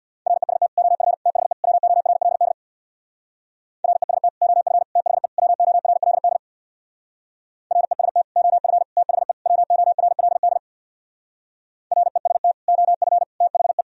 Morse code, heard as a keyed tone, about 64 wpm, 700 Hz. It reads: REST OF THE WORLD REST OF THE WORLD REST OF THE WORLD REST OF THE